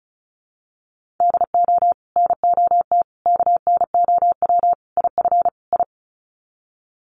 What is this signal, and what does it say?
Morse code, heard as a keyed tone, about 34 wpm, 700 Hz. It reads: DO NOT KNOW IF I
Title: Sets of 5 Words 35wpm